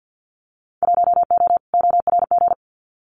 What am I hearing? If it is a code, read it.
Morse code, heard as a keyed tone, about 50 wpm, 700 Hz. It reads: 1O ORG
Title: Winter Field Day Exchanges 50wpm